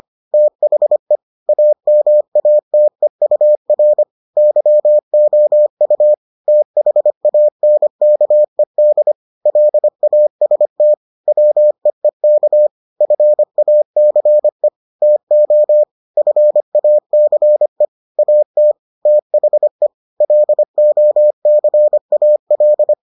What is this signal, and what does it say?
Morse code, heard as a keyed tone, about 25 wpm, 600 Hz. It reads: THE AMATEUR YOU THANKED LAST WEEK FACE TO FACE AT THE LOCAL